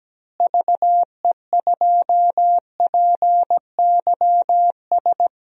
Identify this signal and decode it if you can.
Morse code, heard as a keyed tone, about 17 wpm, 700 Hz. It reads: VE2PYS
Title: DXCC-100 Call Signs - 6 Characters - Repeated 3x - 17wpm